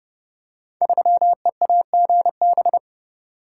Morse code, heard as a keyed tone, about 30 wpm, 700 Hz. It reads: 3EAGB
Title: Code Groups - 5 Letter:Numbers - 30wpm